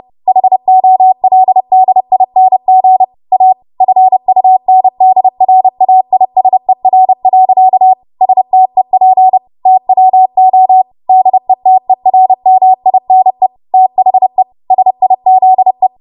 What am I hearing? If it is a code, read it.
Morse code, heard as a keyed tone, about 30 wpm, 750 Hz. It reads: HOLDING A FUNDRAISER. STEP TWO DETERMINE THE SIZE
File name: Morse / 30WPM